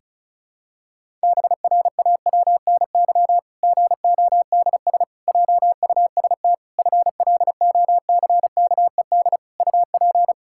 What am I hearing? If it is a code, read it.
Morse code, heard as a keyed tone, about 36 wpm, 700 Hz. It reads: BRAWNY GODS JUST FLOCKED UP
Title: Pangram Challenge v2 - 35wpm